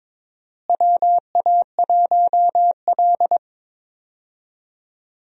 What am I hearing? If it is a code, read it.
Morse code, heard as a keyed tone, about 22 wpm, 700 Hz. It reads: WA1L
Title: USA Call Signs - 4 Characters - Repeated 3x - 22wpm